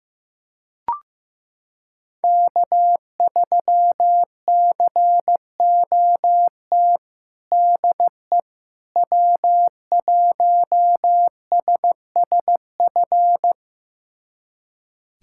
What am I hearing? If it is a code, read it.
Morse code, heard as a keyed tone, about 14 wpm, 700 Hz. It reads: K3COT DE W1SSF
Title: USA Callsign DE Callsign 15wpm